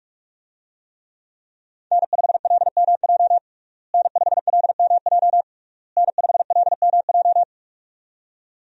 Morse code, heard as a keyed tone, about 47 wpm, 700 Hz. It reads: N5LMJ N5LMJ N5LMJ
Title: USA Call Signs - 5 Characters - Repeated 3x - 45wpm